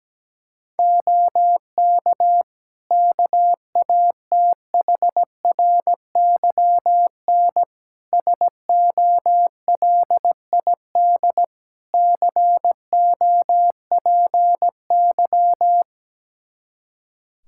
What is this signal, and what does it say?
Morse code, heard as a keyed tone, about 17 wpm, 700 Hz. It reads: OK KATHRYN SOLID COPY
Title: K1USN SST QSO Practice 17wpm